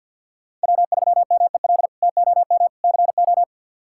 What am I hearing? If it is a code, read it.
Morse code, heard as a keyed tone, about 50 wpm, 690 Hz. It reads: W3MEL TOM CO
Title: K1USN SST QSO Practice - 50wpm